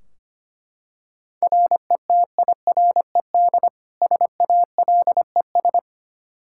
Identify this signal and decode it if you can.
Morse code, heard as a keyed tone, about 25 wpm, 700 Hz. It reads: RETIRED SALES
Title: Practice QSO - Exchange 3 - More Info - 25wpm